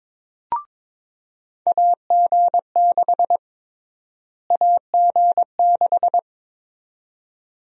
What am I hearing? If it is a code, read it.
Morse code, heard as a keyed tone, about 22 wpm, 700 Hz. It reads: AG6 AG6
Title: USA Call Sign Prefixes - Repeated 2x - 22wpm